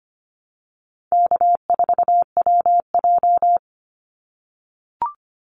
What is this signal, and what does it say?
Morse code, heard as a keyed tone, about 25 wpm, 700 Hz. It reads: K4WJ
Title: Call Signs - 25wpm